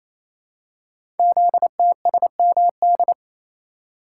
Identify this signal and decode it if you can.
Morse code, heard as a keyed tone, about 27 wpm, 700 Hz. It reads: ZTSMD
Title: Code Groups - 5 Letters - 28wpm